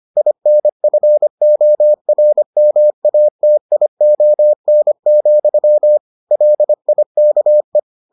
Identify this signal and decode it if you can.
Morse code, heard as a keyed tone, about 25 wpm, 600 Hz. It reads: INFORMATION, LIKE